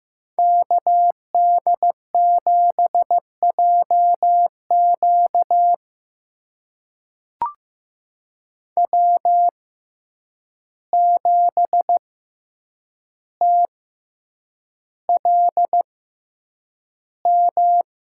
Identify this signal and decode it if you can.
Morse code, heard as a keyed tone, about 15 wpm, spaced as for 8 wpm, 700 Hz. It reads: KD7JQ W7TLM
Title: USA Call Signs - 5 Characters - 6x Character Spacing - 15wpm